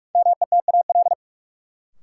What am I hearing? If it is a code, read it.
Morse code, heard as a keyed tone, about 46 wpm, 700 Hz. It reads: METAL
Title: Sentences from Top 1000 Words - Review Part 6 - New Word Drill - 45wpm